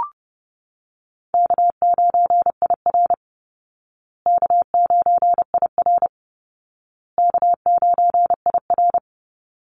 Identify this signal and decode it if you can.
Morse code, heard as a keyed tone, about 30 wpm, 700 Hz. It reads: K9IR K9IR K9IR
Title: Call Signs Repeated 2x - 30wpm